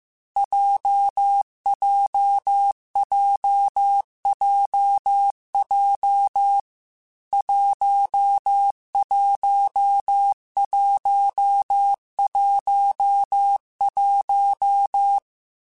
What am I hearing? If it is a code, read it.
Morse code, heard as a keyed tone, about 15 wpm, 800 Hz. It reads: JJJJJ 11111